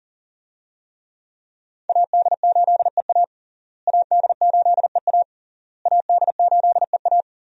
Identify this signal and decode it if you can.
Morse code, heard as a keyed tone, about 40 wpm, 700 Hz. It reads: AD8EA AD8EA AD8EA
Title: USA Call Signs - 5 Characters - Repeated 3x - 40wpm